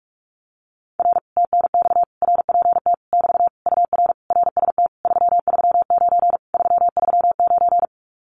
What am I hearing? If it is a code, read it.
Morse code, heard as a keyed tone, about 46 wpm, 700 Hz. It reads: R TNX RPT <BT> UR RST 339 339
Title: Common QSO Phrases Encore 45wpm